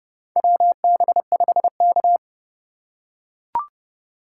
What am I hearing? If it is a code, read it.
Morse code, heard as a keyed tone, about 29 wpm, 700 Hz. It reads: WB5K
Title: K1USN SST QSO Practice - 30wpm